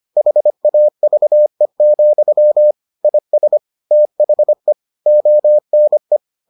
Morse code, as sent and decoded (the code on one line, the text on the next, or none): .... .- ...- . --..-- / .. ... / - .... / --- -. .
HAVE, IS THE ONE